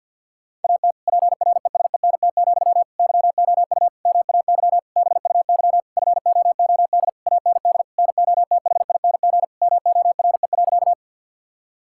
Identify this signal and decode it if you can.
Morse code, heard as a keyed tone, about 50 wpm, 700 Hz. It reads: AT PRESENT, YOU MAY BUY FOOD AND NOTHING MORE.
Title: Sentences from Top 1000 Words - Review Part 5 - 50wpm